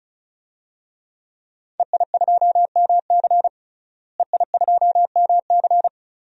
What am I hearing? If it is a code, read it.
Morse code, heard as a keyed tone, about 36 wpm, 700 Hz. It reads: EI2MC EI2MC
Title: DXCC-100 Call Signs - 5 Characters - Repeated 2x - 35wpm